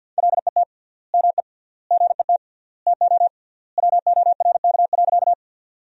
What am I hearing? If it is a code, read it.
Morse code, heard as a keyed tone, about 50 wpm, 700 Hz. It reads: LET ME GET TO WORK.
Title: 3x Word-Spacing - Sentences from Top 300 Words - 50wpm